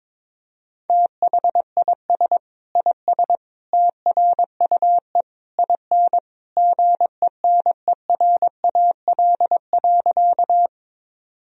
Morse code, heard as a keyed tone, about 22 wpm, 700 Hz. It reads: THIS IS TRUE IN GENERAL.
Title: Sentences from Top 1000 Words - Review Part 3 - 22wpm